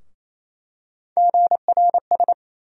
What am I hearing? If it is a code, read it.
Morse code, heard as a keyed tone, about 28 wpm, 700 Hz. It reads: GRS
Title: USA Call Sign Postfixes - 3 Letters 28wpm